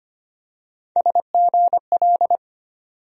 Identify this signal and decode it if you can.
Morse code, heard as a keyed tone, about 25 wpm, 700 Hz. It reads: SGL
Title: USA Call Sign Postfixes - 3 Letters 25wpm